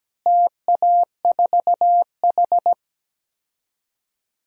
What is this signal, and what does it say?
Morse code, heard as a keyed tone, about 17 wpm, 700 Hz. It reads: TA4H
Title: DXCC-100 Call Signs - 4 Characters - Repeated 3x - 17wpm